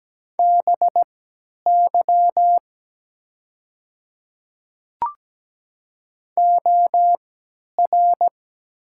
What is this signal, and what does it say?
Morse code, heard as a keyed tone, about 17 wpm, spaced as for 10 wpm, 700 Hz. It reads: BY OR
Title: 3x Character Spacing - Top 100 Words - 17wpm